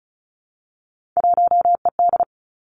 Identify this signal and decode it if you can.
Morse code, heard as a keyed tone, about 36 wpm, 700 Hz. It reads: 1ED
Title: Three Challenging Letter-Number Combinations 35wpm